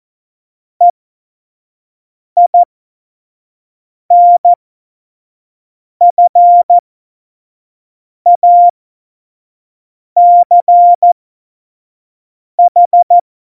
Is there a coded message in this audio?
Yes: Morse code from 0.8 to 13.2 s, heard as a keyed tone, about 14 wpm, spaced as for 5 wpm, 700 Hz. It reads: EINFACH